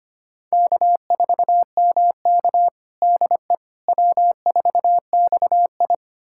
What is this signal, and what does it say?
Morse code, heard as a keyed tone, about 25 wpm, 700 Hz. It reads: K4MK DE W4XI
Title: USA Callsign DE Callsign - 4 Characters - 25wpm